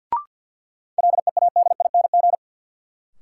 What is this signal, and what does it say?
Morse code, heard as a keyed tone, about 53 wpm, 700 Hz. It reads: LEADING
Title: Sentences from Top 1100 Words - New Word Drill Alt - 50wpm